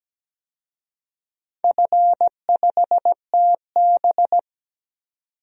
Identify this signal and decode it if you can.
Morse code, heard as a keyed tone, about 17 wpm, 700 Hz. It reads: F5TB
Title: DXCC-100 Call Signs - 4 Characters - 17wpm